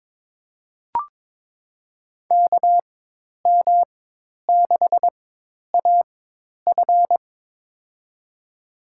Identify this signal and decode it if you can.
Morse code, heard as a keyed tone, about 22 wpm, spaced as for 10 wpm, 700 Hz. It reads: KM6AF
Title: USA Call Signs - 5 Characters - 4x Character Spacing - 22wpm